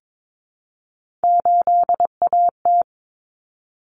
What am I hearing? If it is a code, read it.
Morse code, heard as a keyed tone, about 22 wpm, 700 Hz. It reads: 8AT
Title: Common Cut Numbers - 3-Digit 22wpm